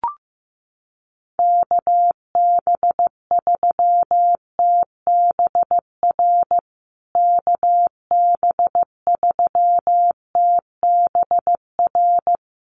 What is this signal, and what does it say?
Morse code, heard as a keyed tone, about 14 wpm, 700 Hz. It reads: KB3TBR KB3TBR
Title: USA Call Signs Repeated 3x 15wpm